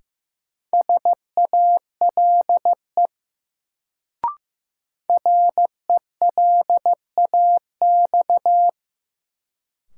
Morse code, heard as a keyed tone, about 15 wpm, 700 Hz. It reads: SALE RELAX
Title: Sentences from Top 1400 Words - New Word Drill Alt - 15wpm